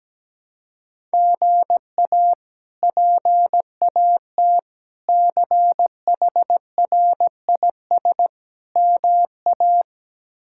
Morse code, heard as a keyed tone, about 17 wpm, 700 Hz. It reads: GA PAT CHRIS MA
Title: K1USN SST QSO Practice - 17wpm